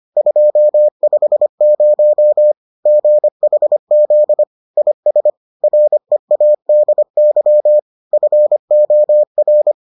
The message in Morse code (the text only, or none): 250 GHZ IS READY FOR